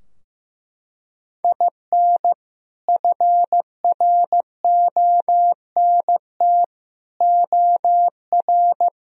Morse code, heard as a keyed tone, about 15 wpm, 700 Hz. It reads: IN FRONT OR
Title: Sentences from Top 1000 Words - Review Part 2 - 15wpm